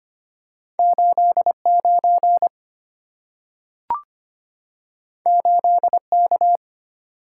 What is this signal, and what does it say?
Morse code, heard as a keyed tone, about 25 wpm, 690 Hz. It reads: 89 8K
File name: Two Letter-Number Combinations - 25wpm